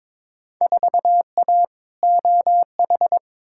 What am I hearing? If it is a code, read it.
Morse code, heard as a keyed tone, about 22 wpm, 700 Hz. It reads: 4A OH
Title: Field Day Exchanges - 22wpm